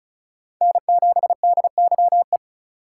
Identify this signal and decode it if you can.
Morse code, heard as a keyed tone, about 44 wpm, 700 Hz. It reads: N7DYE